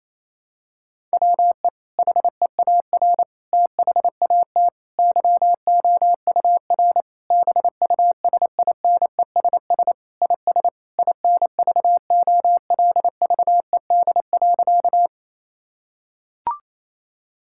Morse code, heard as a keyed tone, about 28 wpm, 700 Hz. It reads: WE HEAR THAT YOUR BUSINESS IS INVOLVED.
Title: Sentences from Top 1000 Words - Review Part 5 - 28wpm